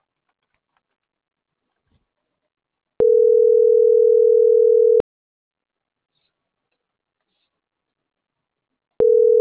Ringback tone (North America), 440 + 480 Hz.